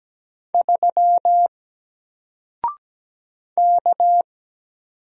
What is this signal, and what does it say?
Morse code, heard as a keyed tone, about 17 wpm, 700 Hz. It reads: 3 K
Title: Single Letter-Number Alt 17wpm